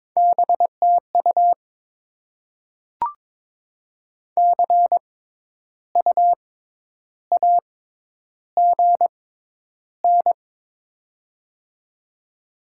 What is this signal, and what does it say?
Morse code, heard as a keyed tone, about 22 wpm, spaced as for 10 wpm, 700 Hz. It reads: BTU CU AGN
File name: Top 20 QSO Elements - 6x Character Spacing - 22wpm